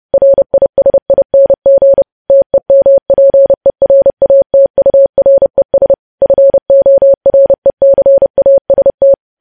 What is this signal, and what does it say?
Morse code, heard as a keyed tone, about 30 wpm, 550 Hz. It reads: RISING TEMPERATURES FORECAST
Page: News Headlines in Morse Code at 30 WPM